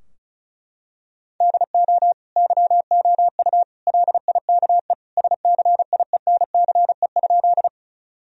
Morse code, heard as a keyed tone, about 34 wpm, 700 Hz. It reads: DO YOU LIKE SCIENCE?
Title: Sentences from Top 1000 Words - Review Part 2 - 35wpm